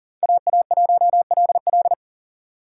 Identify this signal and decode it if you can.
Morse code, heard as a keyed tone, about 37 wpm, 700 Hz. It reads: AA1LL
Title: USA Call Signs - 5 Characters - 4x Character Spacing - 40wpm